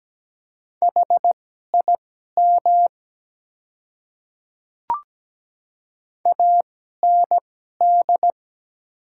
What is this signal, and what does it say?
Morse code, heard as a keyed tone, about 17 wpm, spaced as for 12 wpm, 700 Hz. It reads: HIM AND
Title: 2x Character Spacing - Top 100 Words - 17wpm